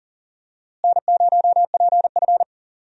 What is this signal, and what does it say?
Morse code, heard as a keyed tone, about 40 wpm, 700 Hz. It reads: N0PF
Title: USA Call Signs - 4 Characters - Repeated 2x - 40wpm